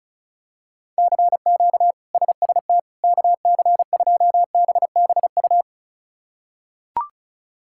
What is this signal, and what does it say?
Morse code, heard as a keyed tone, about 34 wpm, 700 Hz. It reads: CQ SST KC2BBU
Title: K1USN SST QSO Practice - 35wpm